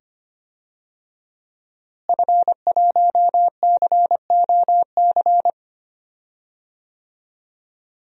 Morse code, heard as a keyed tone, about 25 wpm, 700 Hz. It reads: F1COC
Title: DXCC-100 Call Signs - 5 Characters - Repeated 2x - 25wpm